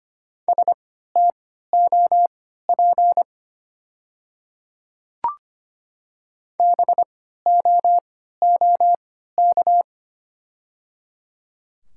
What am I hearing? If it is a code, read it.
Morse code, heard as a keyed tone, about 25 wpm, spaced as for 14 wpm, 700 Hz. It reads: STOP BOOK
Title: 3x Character Spacing - Top 300 Words - 25wpm